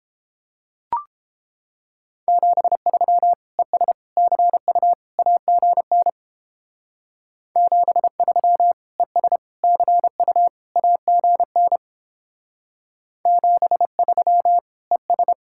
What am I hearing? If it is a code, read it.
Morse code, heard as a keyed tone, about 29 wpm, 700 Hz. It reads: 73 ES CU AGN 73 ES CU AGN 73 ES